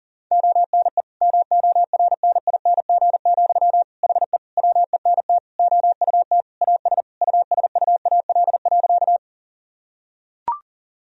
Morse code, heard as a keyed tone, about 40 wpm, 700 Hz. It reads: ONE MORNING, HE WENT OUT AS USUAL.
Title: Sentences from Top 1000 Words - Review Part 2 - 40wpm